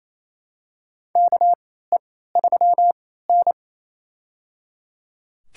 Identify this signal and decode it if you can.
Morse code, heard as a keyed tone, about 28 wpm, spaced as for 16 wpm, 700 Hz. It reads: KE3N
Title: USA Call Signs - 4 Characters - 3x Character Spacing - 28wpm